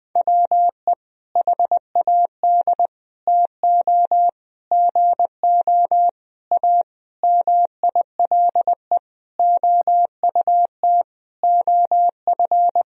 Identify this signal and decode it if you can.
Morse code, heard as a keyed tone, about 20 wpm, 700 Hz. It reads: WE HAD TO GO A MILE OUT OF
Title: Sentences from Top 1000 Words - Review Part 1 - 20wpm